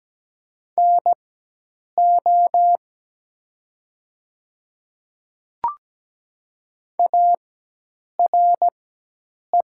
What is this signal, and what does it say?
Morse code, heard as a keyed tone, about 17 wpm, spaced as for 8 wpm, 700 Hz. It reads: NO ARE
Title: 4x Character Spacing - Top 100 Words - 17wpm